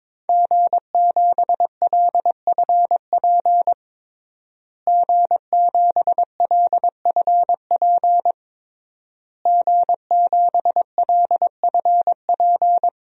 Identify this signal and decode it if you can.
Morse code, heard as a keyed tone, about 21 wpm, 700 Hz. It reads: G7LFP G7LFP G7LFP
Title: DXCC-100 Call Signs - 5 Characters - Repeated 3x - 22wpm